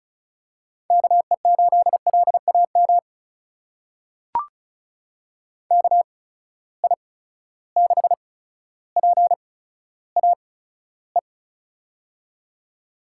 Morse code, heard as a keyed tone, about 36 wpm, spaced as for 15 wpm, 700 Hz. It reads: KE8LAM KI6PAE
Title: USA Call Signs - 6 Characters - 8x Character Spacing - 35wpm